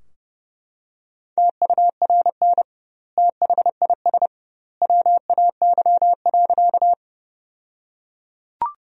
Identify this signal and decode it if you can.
Morse code, heard as a keyed tone, about 30 wpm, 700 Hz. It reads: TURN THIS WAY.
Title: 2x Word-Spacing - Sentences from Top 200 Words - 30wpm